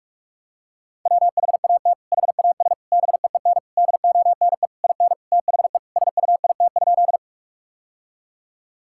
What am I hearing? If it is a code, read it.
Morse code, heard as a keyed tone, about 46 wpm, 700 Hz. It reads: WHAT HAS BEEN DONE IN THE SUIT?
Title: Sentences from Top 1600 Words 45wpm